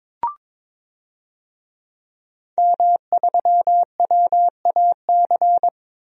Morse code, heard as a keyed tone, about 22 wpm, 700 Hz. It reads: M3WAC
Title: DXCC-100 Call Signs - 5 Characters - 22wpm